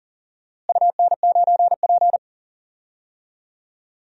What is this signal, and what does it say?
Morse code, heard as a keyed tone, about 39 wpm, 700 Hz. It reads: UN9P
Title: DXCC-100 Call Signs - 4 Characters - Repeated 2x - 40wpm